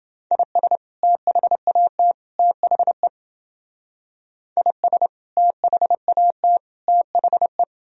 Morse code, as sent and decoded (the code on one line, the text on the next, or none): .. ... / - .... .- - / - .... / .. ... / - .... .- - / - .... .
IS THAT THE IS THAT THE